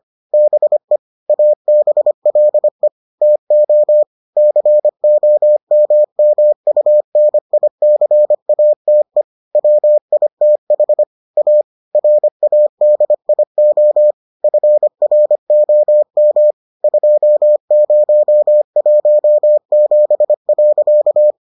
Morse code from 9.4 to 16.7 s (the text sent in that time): WITH A RADIO FROM